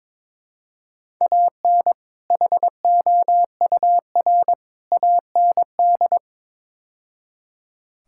Morse code, heard as a keyed tone, about 22 wpm, 700 Hz. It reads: AN HOUR AND
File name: practice / 22wpm / Build A Phrase - 5-Word Phrases - 22wpm